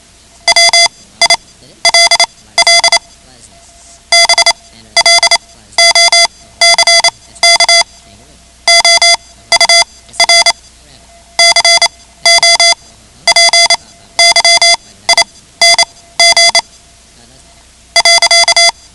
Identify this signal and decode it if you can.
Morse code, heard as a keyed tone, about 28 wpm, spaced as for 14 wpm, 2050 Hz. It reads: WILL BLOCK OUR COPYING.
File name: ham radio / CodeQuick / Code Quick 10